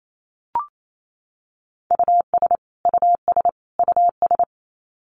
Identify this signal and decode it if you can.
Morse code, heard as a keyed tone, about 28 wpm, 700 Hz. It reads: US US US
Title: Top 500 Words Repeated 28wpm